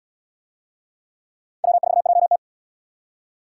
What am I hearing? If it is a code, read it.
Morse code, heard as a keyed tone, about 73 wpm, 700 Hz. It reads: W4JT